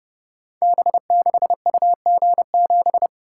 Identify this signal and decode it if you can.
Morse code, heard as a keyed tone, about 30 wpm, 700 Hz. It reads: B6UG7
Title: Code Groups - 5 Letter:Numbers - 30wpm